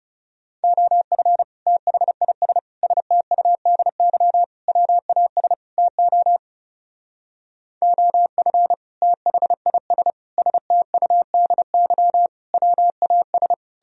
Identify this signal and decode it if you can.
Morse code, heard as a keyed tone, about 35 wpm, 700 Hz. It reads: OF THIS STUDY WAS TO OF THIS STUDY WAS
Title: Speed-Racing - Sets of 5 Words 30wpm